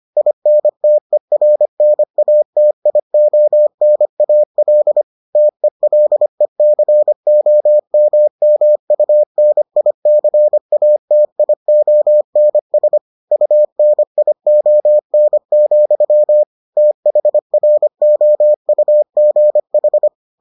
Morse code, heard as a keyed tone, about 25 wpm, 600 Hz. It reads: INTERNATIONAL TELECOMMUNICATIONS UNION, THROUGH